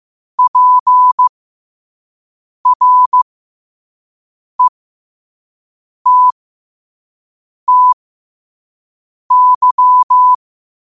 Morse code, heard as a keyed tone, about 15 wpm, spaced as for 5 wpm, 1000 Hz. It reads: PRETTY
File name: qotdfiles / 5wpm